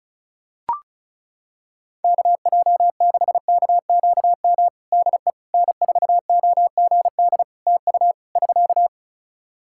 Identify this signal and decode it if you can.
Morse code, heard as a keyed tone, about 36 wpm, 700 Hz. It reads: KJ6KQM DE N4OGD TU <SK>
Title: Practice QSO - Exchange 4 - Ending - 35wpm